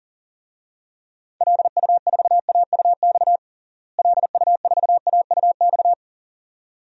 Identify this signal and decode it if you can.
Morse code, heard as a keyed tone, about 40 wpm, 700 Hz. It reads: LU4AUX LU4AUX
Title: DXCC-100 Call Signs - 6 Characters - Repeated 2x - 40wpm